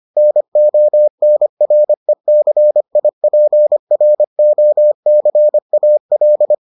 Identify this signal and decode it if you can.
Morse code, heard as a keyed tone, about 25 wpm, 600 Hz. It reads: NONRECIPROCAL